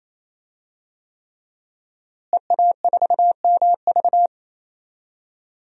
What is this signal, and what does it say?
Morse code, heard as a keyed tone, about 28 wpm, 700 Hz. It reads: EA4MV
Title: DXCC-100 Call Signs - 5 Characters - Repeated 3x - 28wpm